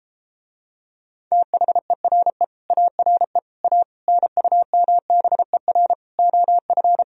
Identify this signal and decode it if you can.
Morse code, heard as a keyed tone, about 33 wpm, 700 Hz. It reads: THERE ARE A NUMBER OF